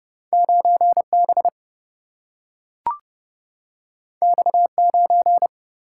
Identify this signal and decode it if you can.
Morse code, heard as a keyed tone, about 30 wpm, 700 Hz. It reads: 9 B X 9
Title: Single-Letter Single-Number 30wpm